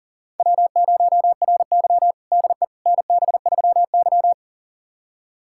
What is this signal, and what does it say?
Morse code, heard as a keyed tone, about 40 wpm, 700 Hz. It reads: W0RY DE NB3Y
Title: USA Callsign DE Callsign - 4 Characters - 40wpm